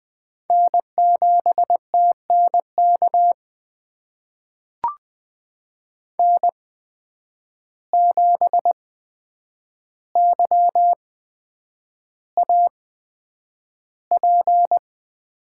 Morse code, heard as a keyed tone, about 20 wpm, spaced as for 9 wpm, 700 Hz. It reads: N7TNK N7YAP
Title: USA Call Signs - 5 Characters - 8x Character Spacing - 20wpm